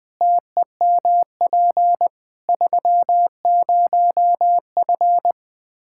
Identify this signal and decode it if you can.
Morse code, heard as a keyed tone, about 20 wpm, 700 Hz. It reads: TEMP 30F
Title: K1USN SST QSO Practice 20wpm